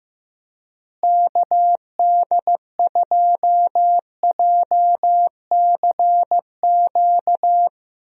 Morse code, heard as a keyed tone, about 15 wpm, 700 Hz. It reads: KD2JCQ
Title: USA Call Signs - 6 Characters - Repeated 3x - 15wpm